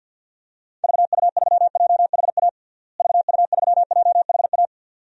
Morse code, heard as a keyed tone, about 50 wpm, 700 Hz. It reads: VU3JHA VU3JHA
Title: DXCC-100 Call Signs - 6 Characters - Repeated 2x - 50wpm